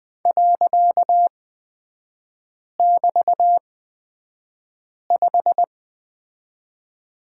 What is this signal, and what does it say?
Morse code, heard as a keyed tone, about 20 wpm, spaced as for 5 wpm, 700 Hz. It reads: .<BT>5
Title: 255 Lesson 41e - Period - Three Characters 20wpm